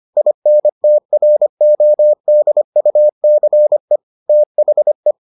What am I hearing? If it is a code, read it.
Morse code, heard as a keyed tone, about 25 wpm, 600 Hz. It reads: INTRODUCE THE